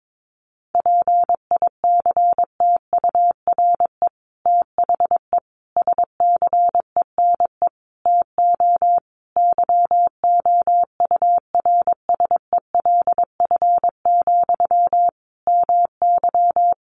Morse code, heard as a keyed tone, about 22 wpm, 700 Hz. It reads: PICTURE THE SCENE TO YOURSELF, MY
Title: Sentences from Top 1100 Words 22wpm